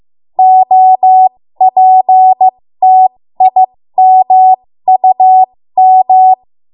Morse code, heard as a keyed tone, about 15 wpm, spaced as for 13 wpm, 750 Hz. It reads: OPTIMUM
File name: Morse / 13WPM